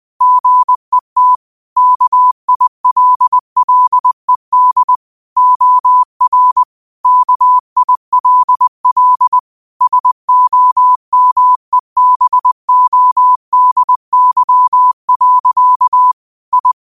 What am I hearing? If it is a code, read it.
Morse code, heard as a keyed tone, about 20 wpm, 1000 Hz. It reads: GET KILLED OR KILL SOMEBODY. I
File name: qotdfiles / 20wpm